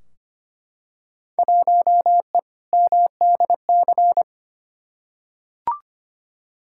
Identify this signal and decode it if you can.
Morse code, heard as a keyed tone, about 25 wpm, 700 Hz. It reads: 1E MDC
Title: Field Day Exchanges Encore - 25wpm